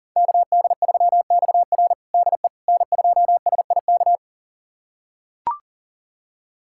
Morse code, heard as a keyed tone, about 40 wpm, 700 Hz. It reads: KD3XR DE N2SIK
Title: USA Callsign DE Callsign - 5 Characters - 40wpm